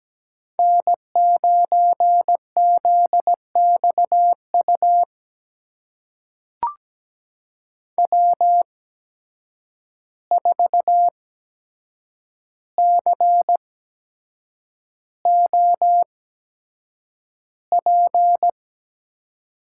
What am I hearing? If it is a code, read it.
Morse code, heard as a keyed tone, about 17 wpm, spaced as for 7 wpm, 700 Hz. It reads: N9ZXU W4COP
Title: USA Call Signs - 5 Characters - 8x Character Spacing - 17wpm